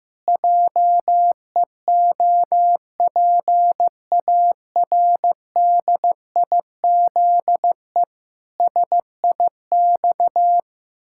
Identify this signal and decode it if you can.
Morse code, heard as a keyed tone, about 16 wpm, 700 Hz. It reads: JEOPARDIZE SIX
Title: Pangram Challenge v2 - 15wpm